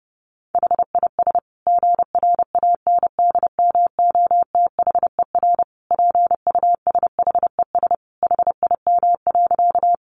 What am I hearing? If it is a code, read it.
Morse code, heard as a keyed tone, about 30 wpm, 700 Hz. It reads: HIS GRANDMOTHER PUSHES HIM.
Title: Additional Easy Phrases 30wpm